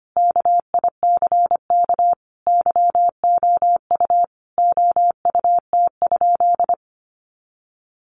Morse code, heard as a keyed tone, about 25 wpm, 700 Hz. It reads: KICK YOU OUT?
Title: Sentences from Top 1300 Words 25wpm